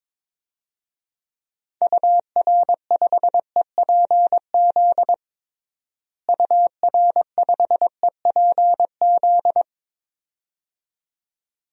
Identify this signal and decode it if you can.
Morse code, heard as a keyed tone, about 22 wpm, 700 Hz. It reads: UR5EPZ UR5EPZ
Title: DXCC-100 Call Signs - 6 Characters - Repeated 2x - 22wpm